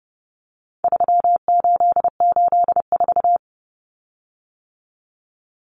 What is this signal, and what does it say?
Morse code, heard as a keyed tone, about 30 wpm, 700 Hz. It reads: 3884
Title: Common Cut Numbers - 4-Digit 30wpm